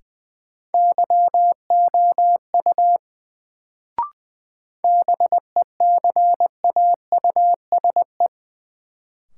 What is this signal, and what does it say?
Morse code, heard as a keyed tone, about 20 wpm, 700 Hz. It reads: YOU BECAUSE
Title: Sets of 2 Words - New Word Drill - 20wpm